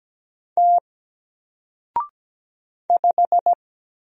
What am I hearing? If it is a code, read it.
Morse code, heard as a keyed tone, about 17 wpm, 700 Hz. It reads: T 5
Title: Single Letter-Number 17wpm